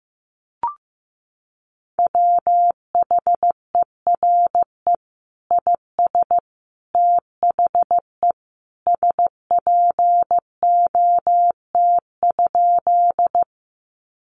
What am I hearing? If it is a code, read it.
Morse code, heard as a keyed tone, about 15 wpm, 700 Hz. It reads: WHERE IS THE SPOT?
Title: Sentences from Top 1100 Words 15wpm